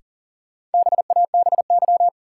Morse code, heard as a keyed tone, about 40 wpm, 700 Hz. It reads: BABY
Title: Sentences from Top 1300 Words - New Word Drill Alt - 40wpm